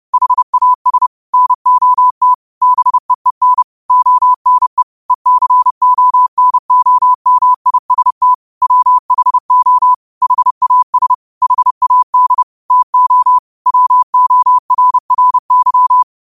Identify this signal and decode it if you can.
Morse code, heard as a keyed tone, about 30 wpm, 1000 Hz. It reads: HAS NOT BEEN ONE ECONOMIST WHO HAS HAD TO WORRY
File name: qotdfiles / 30wpm